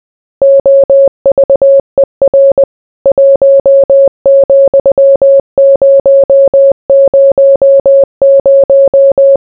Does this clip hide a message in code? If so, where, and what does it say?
Yes: Morse code from 0.4 to 9.4 s, heard as a keyed tone, about 20 wpm, 550 Hz. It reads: OVER 1,000